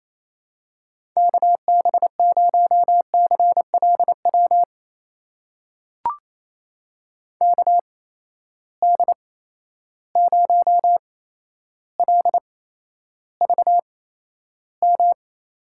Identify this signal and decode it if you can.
Morse code, heard as a keyed tone, about 28 wpm, spaced as for 12 wpm, 700 Hz. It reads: KB0CLW KD0LVM
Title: USA Call Signs - 6 Characters - 8x Character Spacing - 28wpm